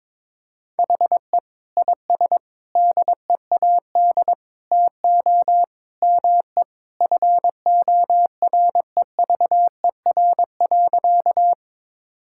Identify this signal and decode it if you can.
Morse code, heard as a keyed tone, about 22 wpm, 700 Hz. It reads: HE IS DEAD TO ME FOREVER.
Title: Sentences from Top 1600 Words 22wpm